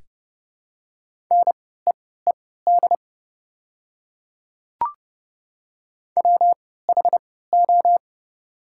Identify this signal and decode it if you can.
Morse code, heard as a keyed tone, about 29 wpm, spaced as for 17 wpm, 690 Hz. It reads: NEED WHO
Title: 3x Character Spacing - Top 300 Words - 30wpm